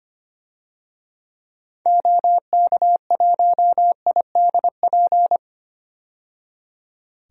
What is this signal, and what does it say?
Morse code, heard as a keyed tone, about 25 wpm, 700 Hz. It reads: OK1IDP
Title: DXCC-100 Call Signs - 6 Characters - 25wpm